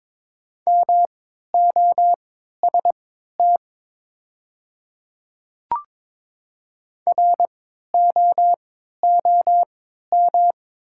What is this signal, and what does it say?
Morse code, heard as a keyed tone, about 22 wpm, spaced as for 12 wpm, 700 Hz. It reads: MOST ROOM
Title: 3x Character Spacing - Top 300 Words - 22wpm